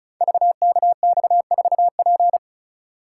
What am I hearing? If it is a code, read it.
Morse code, heard as a keyed tone, about 36 wpm, 700 Hz. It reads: VKX4P
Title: Code Groups - 5 Letter:Numbers - 35wpm